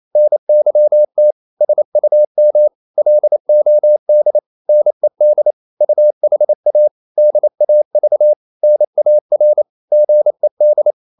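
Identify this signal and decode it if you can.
Morse code, heard as a keyed tone, about 28 wpm, 600 Hz. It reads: NYT SUM LOD NED UHA DAV NAR GED
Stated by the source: Danish short words sent in Morse at 28 wpm, from CW med Gnister.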